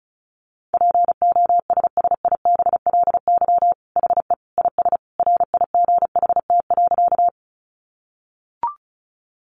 Morse code, heard as a keyed tone, about 34 wpm, 700 Hz. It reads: POSSIBLY HE IS RIGHT.
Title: Sentences from Top 1300 Words 35wpm